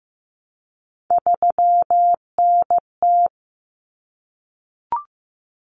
Morse code, heard as a keyed tone, about 15 wpm, 700 Hz. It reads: 3NT
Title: Common Cut Numbers - 3-Digit 15wpm